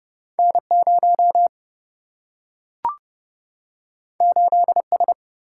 Morse code, heard as a keyed tone, about 30 wpm, 700 Hz. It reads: N 0 8 S
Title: Single-Letter Single-Number 30wpm